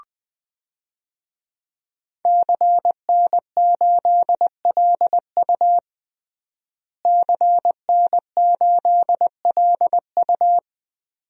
Morse code, heard as a keyed tone, about 21 wpm, 700 Hz. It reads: CN8LU CN8LU
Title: DXCC-100 Call Signs - 5 Characters - Repeated 2x - 20wpm